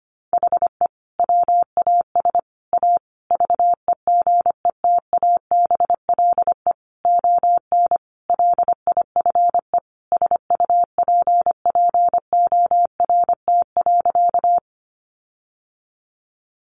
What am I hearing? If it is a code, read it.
Morse code, heard as a keyed tone, about 25 wpm, 700 Hz. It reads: HE WAS A VEGETABLE ON LIFE SUPPORT.
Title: Sentences from Top 1300 Words 25wpm